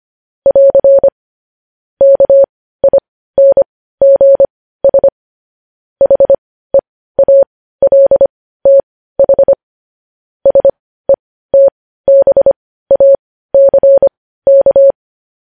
Morse code, heard as a keyed tone, about 25 wpm, spaced as for 15 wpm, 550 Hz. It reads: <AR> KINGS HEALTH SETBACK
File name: podcast / News headlines-1x-25x15